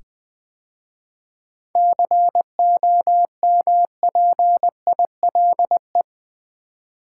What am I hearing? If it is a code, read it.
Morse code, heard as a keyed tone, about 20 wpm, 700 Hz. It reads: COMPILE
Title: 7 Letter Words - First 3 Letters Given - 20wpm